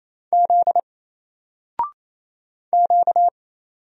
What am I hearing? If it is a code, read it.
Morse code, heard as a keyed tone, about 27 wpm, 700 Hz. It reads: Z Q
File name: Single-Letter-Number-28wpm